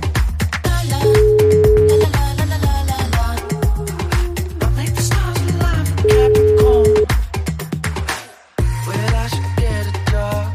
Ringback tone (Europe, 425 Hz).